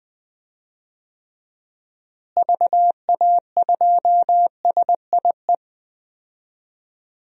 Morse code, heard as a keyed tone, about 20 wpm, 700 Hz. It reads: VA2SIE
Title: DXCC-100 Call Signs - 6 Characters - Repeated 3x - 20wpm